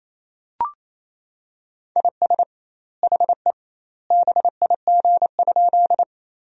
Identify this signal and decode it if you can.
Morse code, heard as a keyed tone, about 28 wpm, 700 Hz. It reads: IS HE BIG?
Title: 2x Word-Spacing - Sentences from Top 200 Words - 28wpm